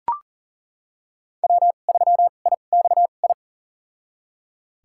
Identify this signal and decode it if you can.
Morse code, heard as a keyed tone, about 40 wpm, spaced as for 29 wpm, 700 Hz. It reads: W3IXI
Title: USA Call Signs - 5 Characters - 2x Character Spacing - 40wpm